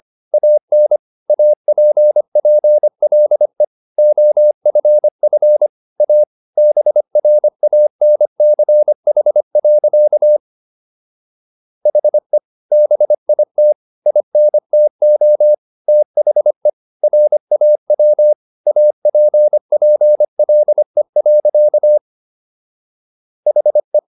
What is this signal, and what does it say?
Morse code, heard as a keyed tone, about 25 wpm, 600 Hz. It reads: AN APPLE OFF A BRANCH. HE BIT INTO THE RAW APPLE. HE